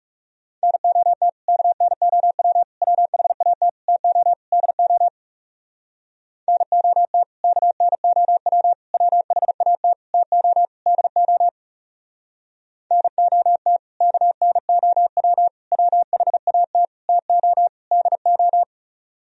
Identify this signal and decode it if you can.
Morse code, heard as a keyed tone, about 37 wpm, 700 Hz. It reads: NOT KNOW WHAT TO DO NOT KNOW WHAT TO DO NOT KNOW WHAT TO DO
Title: Speed-Racing - Sets of 5 Words 30wpm